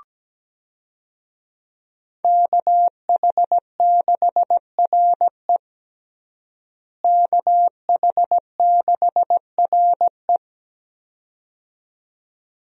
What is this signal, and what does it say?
Morse code, heard as a keyed tone, about 17 wpm, 700 Hz. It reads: KH6RE KH6RE
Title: USA Call Signs - 5 Characters - Repeated 2x - 17wpm